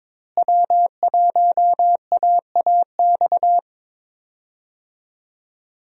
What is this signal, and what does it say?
Morse code, heard as a keyed tone, about 22 wpm, 700 Hz. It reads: W1AAX
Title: USA Call Signs - 5 Characters - Repeated 2x - 22wpm